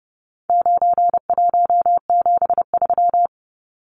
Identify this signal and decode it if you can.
Morse code, heard as a keyed tone, about 30 wpm, 700 Hz. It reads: 9173
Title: Numbers - 4-Digit 30wpm